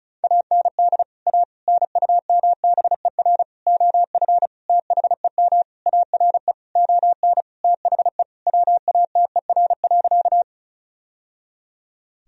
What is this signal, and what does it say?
Morse code, heard as a keyed tone, about 34 wpm, 700 Hz. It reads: AND A NUMBER OF THEM ARE ON THE WATER.
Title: Sentences from Top 1000 Words - Review Part 1 - 35wpm